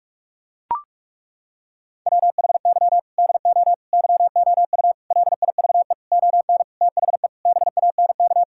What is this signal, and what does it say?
Morse code, heard as a keyed tone, about 41 wpm, 700 Hz. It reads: WHY DO YOU LIVE ON THE BANK
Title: Sentences from Top 1000 Words - Review Part 6 - 45wpm